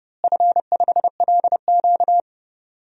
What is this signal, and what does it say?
Morse code, heard as a keyed tone, about 30 wpm, 700 Hz. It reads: F5LQ
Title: DXCC-100 Call Signs - 4 Characters - Repeated 3x - 30wpm